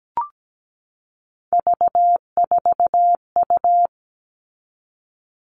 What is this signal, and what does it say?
Morse code, heard as a keyed tone, about 17 wpm, 700 Hz. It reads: V4U
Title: All Cut Numbers - 3-Digit 17wpm